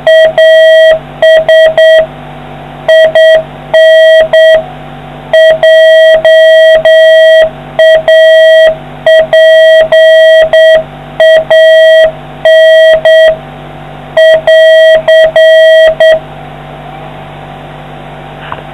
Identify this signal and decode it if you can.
Morse code, heard as a keyed tone, about 8 wpm, 640 Hz. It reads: AS IN JAPAN <AR>